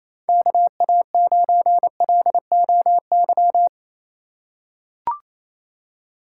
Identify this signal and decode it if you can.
Morse code, heard as a keyed tone, about 28 wpm, 700 Hz. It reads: KA9LOY
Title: USA Call Signs - 2x Character Spacing - 28wpm